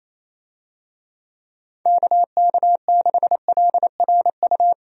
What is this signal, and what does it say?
Morse code, heard as a keyed tone, about 28 wpm, 700 Hz. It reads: KK6LRU
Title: USA Call Signs - 6 Characters - 28wpm